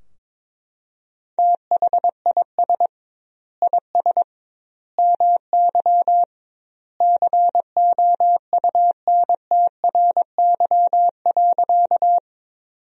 Morse code, heard as a keyed tone, about 22 wpm, 700 Hz. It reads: THIS IS MY COUNTRY.
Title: 2x Word-Spacing - Sentences from Top 300 Words - 22wpm